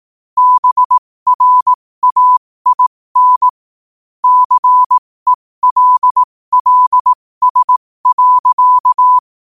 Morse code, heard as a keyed tone, about 18 wpm, spaced as for 15 wpm, 1000 Hz. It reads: BRAIN CELLS.
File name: qotdfiles / 15wpm